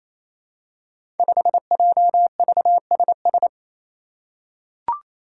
Morse code, heard as a keyed tone, about 28 wpm, 700 Hz. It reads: 5JVSS
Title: Five Challenging Letter-Number Combinations 28wpm